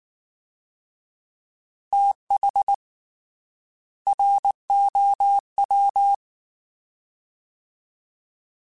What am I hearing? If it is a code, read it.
Morse code, heard as a keyed tone, about 19 wpm, 800 Hz. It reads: TH ROW
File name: Fast TH @18